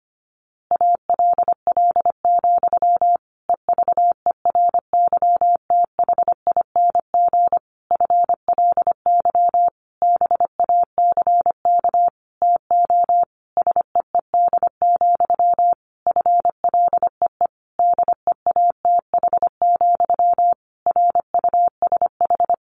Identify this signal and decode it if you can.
Morse code, heard as a keyed tone, about 25 wpm, 700 Hz. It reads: ALL, EVERYTHING FLY BACK TO SEED, FLEE DEATH, RUSH
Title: A Sound Of Thunder 25wpm